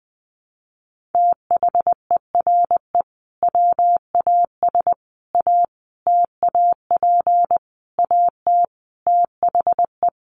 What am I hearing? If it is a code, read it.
Morse code, heard as a keyed tone, about 20 wpm, 700 Hz. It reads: THERE WAS A TAP AT THE
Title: Sentences from Top 1400 Words 20wpm